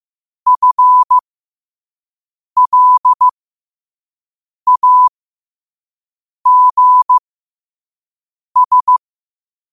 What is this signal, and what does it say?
Morse code, heard as a keyed tone, about 15 wpm, spaced as for 5 wpm, 1000 Hz. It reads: FLAGS
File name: qotdfiles / 5wpm